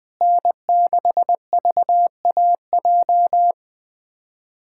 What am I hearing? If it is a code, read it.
Morse code, heard as a keyed tone, about 20 wpm, 700 Hz. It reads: N6VAJ
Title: USA Callsign DE Callsign - 5 Characters - 20wpm